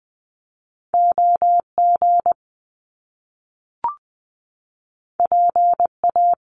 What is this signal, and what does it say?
Morse code, heard as a keyed tone, about 20 wpm, 700 Hz. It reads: OG PA
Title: Top Two Letter Combinations in Words 20wpm